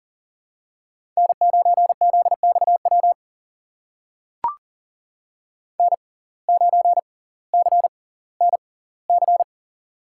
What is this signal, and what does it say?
Morse code, heard as a keyed tone, about 40 wpm, spaced as for 20 wpm, 700 Hz. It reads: N9ZXW N9CNC
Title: USA Call Signs - 5 Characters - 6x Character Spacing - 40wpm